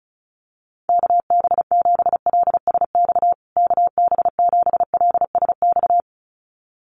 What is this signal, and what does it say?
Morse code, heard as a keyed tone, about 34 wpm, 700 Hz. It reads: KB7LSX KB7LSX
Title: USA Call Signs Repeated 2x 35wpm